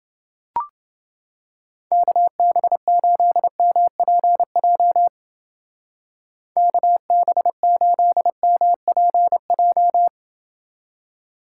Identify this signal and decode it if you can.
Morse code, heard as a keyed tone, about 29 wpm, 700 Hz. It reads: KB8MPJ KB8MPJ